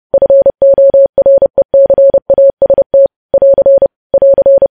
Morse code, heard as a keyed tone, about 30 wpm, 550 Hz. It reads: FORECAST <AR> <AR>